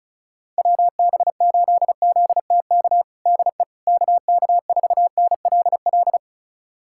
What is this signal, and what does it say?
Morse code, heard as a keyed tone, about 34 wpm, 700 Hz. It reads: WB8ZTK DE KK4NLL
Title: USA Callsign DE Callsign - 6 Characters - 35wpm